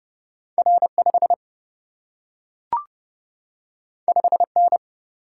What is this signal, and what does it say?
Morse code, heard as a keyed tone, about 30 wpm, 700 Hz. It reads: R 5 5 N
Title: Single-Letter Single-Number 30wpm